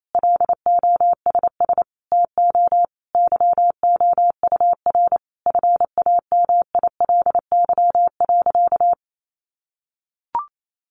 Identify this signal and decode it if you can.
Morse code, heard as a keyed tone, about 28 wpm, 700 Hz. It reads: LOSS TO YOUR FAMILY.
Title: Sentences from Top 1400 Words 28wpm